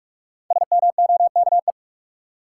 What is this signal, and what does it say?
Morse code, heard as a keyed tone, about 44 wpm, 700 Hz. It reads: SMOKE
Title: Five Letter Words - 45wpm